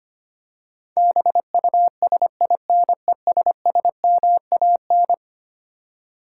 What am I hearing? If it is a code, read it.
Morse code, heard as a keyed tone, about 25 wpm, 700 Hz. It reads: BUSINESSMAN
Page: morse-code-ninja-practice-25wpm